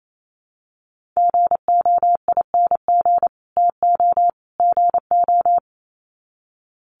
Morse code, heard as a keyed tone, about 28 wpm, 700 Hz. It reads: GOING TO GO
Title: Sets of 3 Words 28wpm